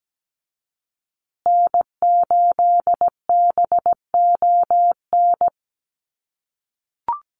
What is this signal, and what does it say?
Morse code, heard as a keyed tone, about 17 wpm, 700 Hz. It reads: N8BON
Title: USA Call Signs Repeated 2x 17wpm